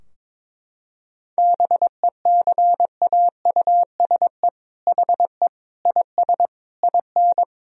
Morse code, heard as a keyed tone, about 22 wpm, 700 Hz. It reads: BECAUSE HE IS IN
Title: Sentences from Top 1000 Words - Review Part 3 - 22wpm